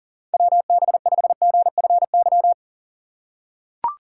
Morse code, heard as a keyed tone, about 40 wpm, 700 Hz. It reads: WB5GFY
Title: USA Call Signs - 3x Character Spacing - 40wpm